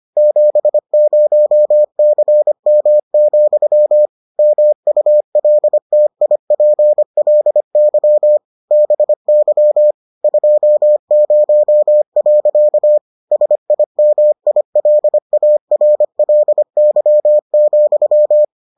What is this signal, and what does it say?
Morse code, heard as a keyed tone, about 25 wpm, 600 Hz. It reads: 70CM, MULTIPLY BY 20. SIMILARLY,